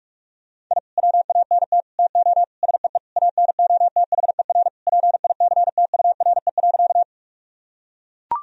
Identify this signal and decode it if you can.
Morse code, heard as a keyed tone, about 41 wpm, 700 Hz. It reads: I WANT TO SEE ANOTHER PICTURE.
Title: Sentences from Top 1000 Words - Review Part 1 - 45wpm